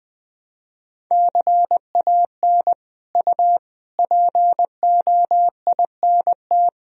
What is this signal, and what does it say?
Morse code, heard as a keyed tone, about 20 wpm, 700 Hz. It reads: CAN U POINT
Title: Sentences from Top 200 Words and Abbreviations 20wpm